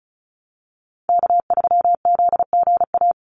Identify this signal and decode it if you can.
Morse code, heard as a keyed tone, about 34 wpm, 700 Hz. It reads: K3ZGA
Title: Call Signs Repeated 3x - 35wpm